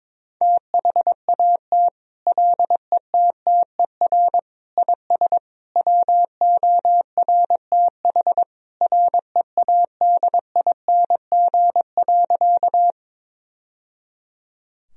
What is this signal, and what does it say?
Morse code, heard as a keyed tone, about 22 wpm, 700 Hz. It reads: THAT LETTER IS WORTH READING.
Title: Sentences from Top 1000 Words - Review Part 3 - 22wpm